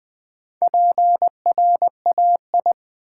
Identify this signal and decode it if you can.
Morse code, heard as a keyed tone, about 20 wpm, 700 Hz. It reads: PRAI
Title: Build A Word - 2 Letters At A Time - 6-Letter Words - 20wpm